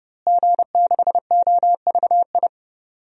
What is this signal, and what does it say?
Morse code, heard as a keyed tone, about 31 wpm, 700 Hz. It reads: G6OVI
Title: DXCC-100 Call Signs - 5 Characters - Repeated 2x - 30wpm